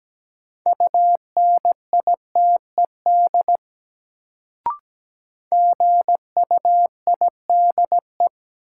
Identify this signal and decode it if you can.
Morse code, heard as a keyed tone, about 17 wpm, 700 Hz. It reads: UNITED GUIDE
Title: Sentences from Top 600 Words - New Word Drill - 17wpm